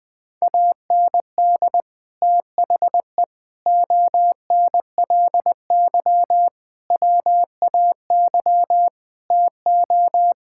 Morse code, heard as a keyed tone, about 21 wpm, 700 Hz. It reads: AND THE ONLY WAY TO
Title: Sets of 9 Words - 20wpm